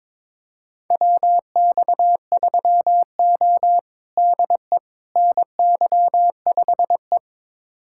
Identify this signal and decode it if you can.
Morse code, heard as a keyed tone, about 22 wpm, 700 Hz. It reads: WX3O DE NY5E
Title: USA Callsign DE Callsign - 4 Characters - 22wpm